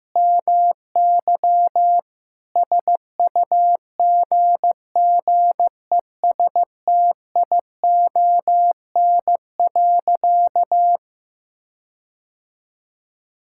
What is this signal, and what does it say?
Morse code, heard as a keyed tone, about 15 wpm, 700 Hz. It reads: MY SUGGESTION.
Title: Sentences from Top 1600 Words 15wpm